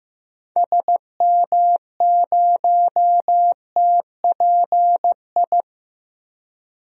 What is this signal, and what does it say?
Morse code, heard as a keyed tone, about 15 wpm, 700 Hz. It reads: SM0TPI
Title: DXCC-100 Call Signs - 6 Characters - Repeated 2x - 15wpm